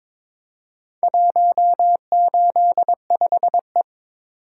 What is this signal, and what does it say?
Morse code, heard as a keyed tone, about 22 wpm, 700 Hz. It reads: 185E
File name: practice / 22wpm / Four Letter-Number Combinations - 22wpm